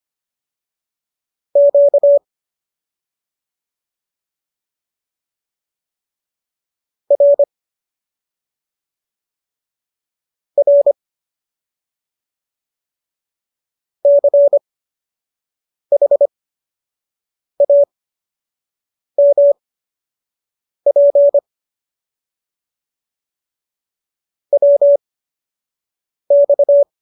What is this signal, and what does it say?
Morse code, heard as a keyed tone, about 25 wpm, spaced as for 4 wpm, 570 Hz. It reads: QRR CHAMP WX